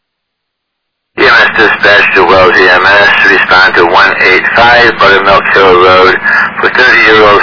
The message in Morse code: none

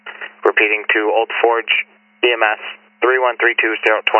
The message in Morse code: none

